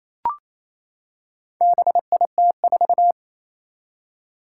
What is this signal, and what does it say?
Morse code, heard as a keyed tone, about 28 wpm, 700 Hz. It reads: BIT4
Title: Four Challenging Letter-Number Combinations 28wpm